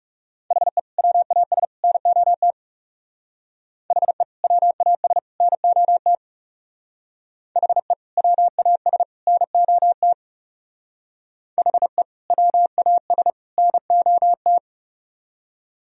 Morse code, heard as a keyed tone, about 36 wpm, 700 Hz. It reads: HE WAS NOT HE WAS NOT HE WAS NOT HE WAS NOT
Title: Speed-Racing - Sets of 3 Words 30wpm